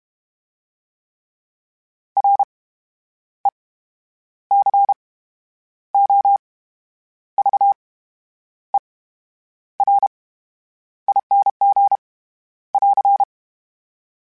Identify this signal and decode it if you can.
Morse code, heard as a keyed tone, about 32 wpm, spaced as for 9 wpm, 800 Hz. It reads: RECOVERING <AR>
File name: Special INGs / Special INGs-mp3s / VerySlowToFaster Ing @30